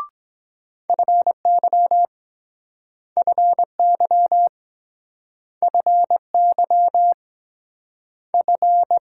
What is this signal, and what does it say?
Morse code, heard as a keyed tone, about 24 wpm, 690 Hz. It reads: FY FY FY F